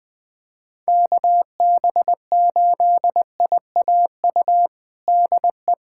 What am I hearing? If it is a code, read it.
Morse code, heard as a keyed tone, about 20 wpm, 700 Hz. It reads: KB8IAU DE
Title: USA Callsign DE Callsign - 6 Characters - 20wpm